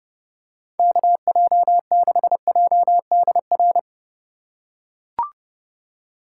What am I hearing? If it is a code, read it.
Morse code, heard as a keyed tone, about 29 wpm, 700 Hz. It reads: KJ6JDR